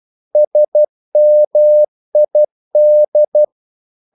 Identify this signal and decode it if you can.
Morse code, heard as a keyed tone, about 12 wpm, 600 Hz. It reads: SMID
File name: Korte ord DK 12wpm